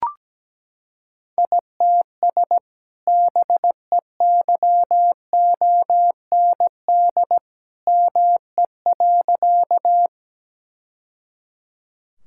Morse code, heard as a keyed tone, about 17 wpm, 700 Hz. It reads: ITS BEYOND ME.
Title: Sentences from Top 1000 Words - Review Part 6 - 17wpm